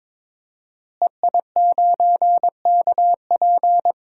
Morse code, heard as a keyed tone, about 21 wpm, 700 Hz. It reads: EI9KP
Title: DXCC-100 Call Signs - 5 Characters - Repeated 2x - 22wpm